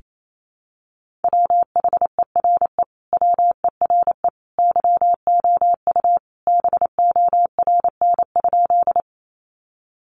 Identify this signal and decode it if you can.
Morse code, heard as a keyed tone, about 28 wpm, 700 Hz. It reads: WHERE WERE YOU BORN?
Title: Sentences from Top 1000 Words 28wpm